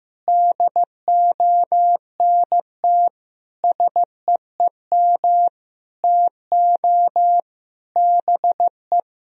Morse code, heard as a keyed tone, about 15 wpm, 700 Hz. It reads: DONT SEEM TO BE
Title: Sentences from Top 1000 Words - Review Part 2 - 15wpm